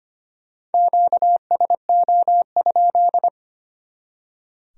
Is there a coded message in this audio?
Yes: Morse code from 0.7 to 3.3 s, heard as a keyed tone, about 33 wpm, 700 Hz. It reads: QSO?